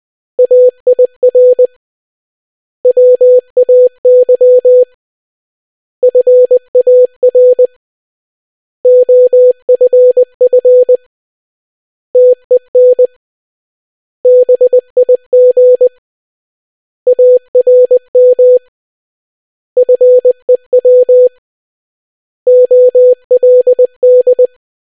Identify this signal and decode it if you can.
Morse code, heard as a keyed tone, about 20 wpm, 500 Hz. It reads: AIR WAY FAR OFF TEN BIG ARM FEW OLD